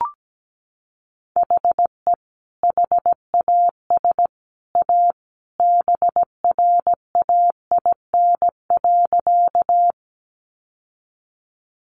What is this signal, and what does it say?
Morse code, heard as a keyed tone, about 17 wpm, 700 Hz. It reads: HE HAS A BRAIN.
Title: Sentences from Top 1100 Words 17wpm